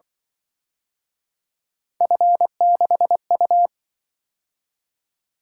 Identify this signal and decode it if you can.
Morse code, heard as a keyed tone, about 23 wpm, 700 Hz. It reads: F6U
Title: Speed-Racing - Three Letter-Number Combinations - 20wpm